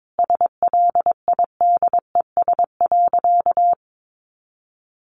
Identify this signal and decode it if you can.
Morse code, heard as a keyed tone, about 22 wpm, 700 Hz. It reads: SLIDES.
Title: Easy Phrases 22wpm